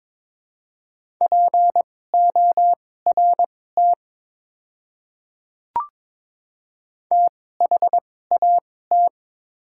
Morse code, heard as a keyed tone, about 21 wpm, spaced as for 16 wpm, 700 Hz. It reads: PORT THAT
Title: 2x Character Spacing - Top 200 Words - 22wpm